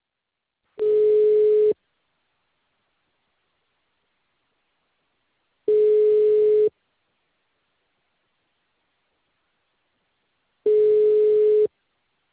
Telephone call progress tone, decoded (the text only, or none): ringback tone